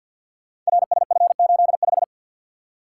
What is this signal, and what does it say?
Morse code, heard as a keyed tone, about 51 wpm, 700 Hz. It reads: RSF85
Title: Code Groups - 5 Letter:Numbers - 50wpm